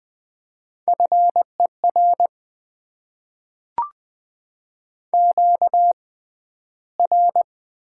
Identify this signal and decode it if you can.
Morse code, heard as a keyed tone, about 20 wpm, spaced as for 12 wpm, 700 Hz. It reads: FER QR
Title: Top 90 QSO Elements - 6x Character Spacing - 20wpm